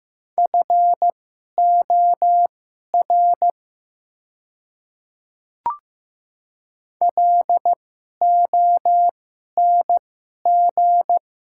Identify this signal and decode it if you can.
Morse code, heard as a keyed tone, about 15 wpm, spaced as for 11 wpm, 700 Hz. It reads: FOR LONG
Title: 2x Character Spacing - Top 300 Words - 15wpm